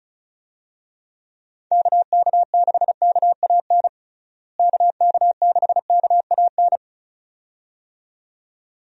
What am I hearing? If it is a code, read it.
Morse code, heard as a keyed tone, about 36 wpm, 700 Hz. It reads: KK6KAN KK6KAN
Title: USA Call Signs - 6 Characters - Repeated 2x - 35wpm